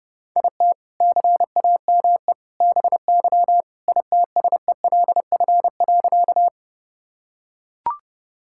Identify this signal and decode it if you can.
Morse code, heard as a keyed tone, about 32 wpm, 700 Hz. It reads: IT CAME BY ITSELF.
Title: Sentences from Top 1000 Words - Review Part 5 - 30wpm